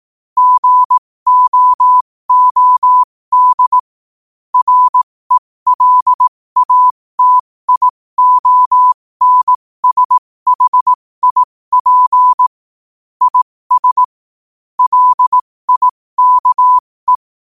Morse code, heard as a keyed tone, about 18 wpm, spaced as for 15 wpm, 1000 Hz. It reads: GOOD RELATIONSHIP IS LIKE
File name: 15wpm